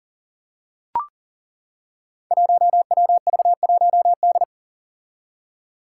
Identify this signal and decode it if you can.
Morse code, heard as a keyed tone, about 41 wpm, 700 Hz. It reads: 1WV1D
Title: Five Challenging Letter-Number Combinations 40wpm